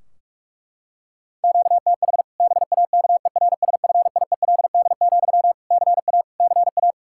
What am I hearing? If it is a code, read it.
Morse code, heard as a keyed tone, about 41 wpm, 700 Hz. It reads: QTH BAKERSFIELD, CA CA
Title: Practice QSO - Exchange 1 - Beginning - 45wpm